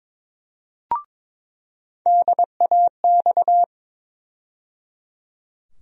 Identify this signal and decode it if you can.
Morse code, heard as a keyed tone, about 22 wpm, 700 Hz. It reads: DAX
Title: USA Call Sign Postfixes - 3 Letters 22wpm